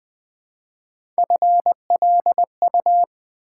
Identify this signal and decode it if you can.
Morse code, heard as a keyed tone, about 20 wpm, 700 Hz. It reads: FLU
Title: Build A Word - 1 Letter At A Time - 3 to 5-Letter Words - 20wpm